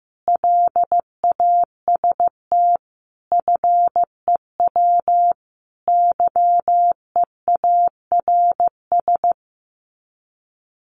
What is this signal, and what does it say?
Morse code, heard as a keyed tone, about 15 wpm, 700 Hz. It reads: LAST FEW YEARS
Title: Sets of 5 Words Encore 15wpm